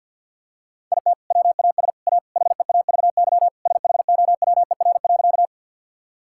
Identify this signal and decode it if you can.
Morse code, heard as a keyed tone, about 52 wpm, 700 Hz. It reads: IT WAS A HEAVY SHOWER.
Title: Sentences from Top 1600 Words 50wpm